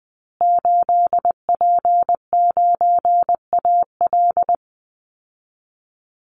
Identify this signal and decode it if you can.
Morse code, heard as a keyed tone, about 20 wpm, 700 Hz. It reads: 8P9AL
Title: Call Signs Repeated 2x - 20wpm